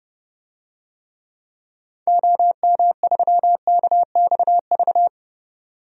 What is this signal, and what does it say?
Morse code, heard as a keyed tone, about 30 wpm, 700 Hz. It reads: OM3KXV
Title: DXCC-100 Call Signs - 6 Characters - 30wpm